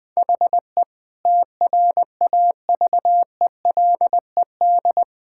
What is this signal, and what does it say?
Morse code, heard as a keyed tone, about 20 wpm, 700 Hz. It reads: HE TRAVELED
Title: Sentences with 5-Word Phrases - Spoken Beginning - 20wpm